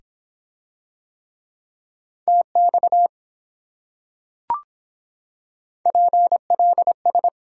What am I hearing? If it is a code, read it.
Morse code, heard as a keyed tone, about 24 wpm, 700 Hz. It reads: TX PLS